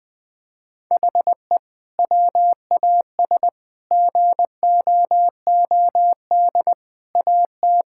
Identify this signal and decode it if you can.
Morse code, heard as a keyed tone, about 21 wpm, 700 Hz. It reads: HE WAS GOOD AT
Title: Sentences from Top 1000 Words - Review Part 5 - 20wpm